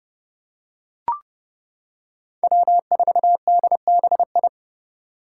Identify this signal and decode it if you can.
Morse code, heard as a keyed tone, about 30 wpm, 700 Hz. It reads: W4DBI
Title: Five Challenging Letter-Number Combinations 30wpm